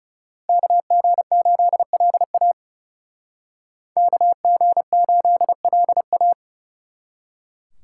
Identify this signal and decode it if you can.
Morse code, heard as a keyed tone, about 34 wpm, 700 Hz. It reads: KG8LA KG8LA
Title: Speed Racing - USA Call Signs 30wpm